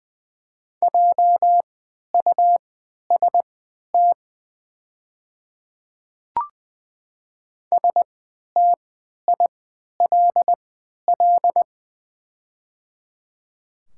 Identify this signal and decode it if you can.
Morse code, heard as a keyed tone, about 20 wpm, spaced as for 11 wpm, 700 Hz. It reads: JUST STILL
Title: 3x Character Spacing - Top 300 Words - 20wpm